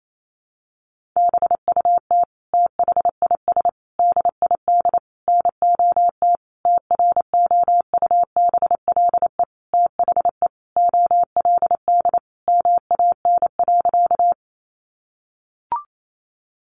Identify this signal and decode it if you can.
Morse code, heard as a keyed tone, about 28 wpm, 700 Hz. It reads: BUT THIS DID NOT TROUBLE THE OLD MAN.
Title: Sentences from Top 1200 Words 28wpm